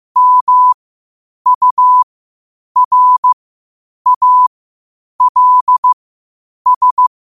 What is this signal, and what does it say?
Morse code, heard as a keyed tone, about 15 wpm, spaced as for 8 wpm, 1000 Hz. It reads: MURALS
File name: qotdfiles / 8wpm